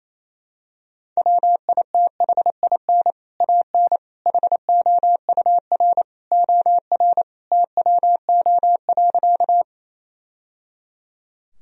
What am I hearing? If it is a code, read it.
Morse code, heard as a keyed tone, about 28 wpm, 700 Hz. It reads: WITHIN AN HOUR OR TWO.
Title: Sentences from Top 1000 Words - Review Part 4 - 28wpm